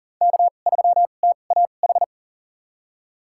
Morse code, heard as a keyed tone, about 40 wpm, spaced as for 29 wpm, 700 Hz. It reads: K3TAH
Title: USA Call Signs - 5 Characters - 2x Character Spacing - 40wpm